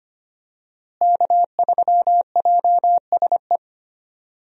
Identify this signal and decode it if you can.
Morse code, heard as a keyed tone, about 25 wpm, 700 Hz. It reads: K3JSE
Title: USA Call Signs - 5 Characters - Repeated 3x - 25wpm